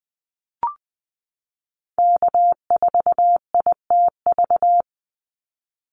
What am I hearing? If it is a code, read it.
Morse code, heard as a keyed tone, about 20 wpm, 700 Hz. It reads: K4ITV
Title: Call Signs Repeated 3x - 20wpm